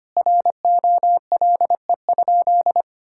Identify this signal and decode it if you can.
Morse code, heard as a keyed tone, about 25 wpm, 700 Hz. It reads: ROLE?
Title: Sentences from Top 1000 Words - Review Part 6 - 25wpm